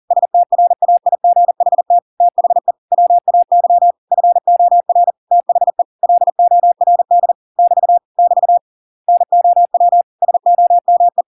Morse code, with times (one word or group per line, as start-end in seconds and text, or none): 0.1-2.0 STRAIGHT
2.2-2.7 THE
2.9-3.9 WAY
4.1-5.1 FOR
5.3-5.8 THE
6.0-7.3 LORD
7.6-8.0 <BT>
8.2-8.6 <BT>
9.1-10.0 NOW
10.2-11.2 SOME